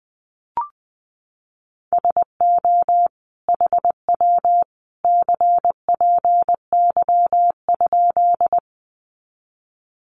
Morse code, heard as a keyed tone, about 20 wpm, 700 Hz. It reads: SO HW CPY?
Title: Common QSO Phrases 20wpm